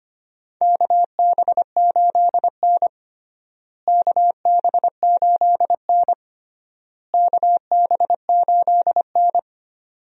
Morse code, heard as a keyed tone, about 25 wpm, 700 Hz. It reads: KB8N KB8N KB8N
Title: USA Call Signs - 4 Characters - Repeated 3x - 25wpm